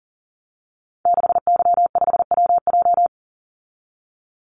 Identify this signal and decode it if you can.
Morse code, heard as a keyed tone, about 40 wpm, 700 Hz. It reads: 6Y5WJ
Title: Call Signs - 40wpm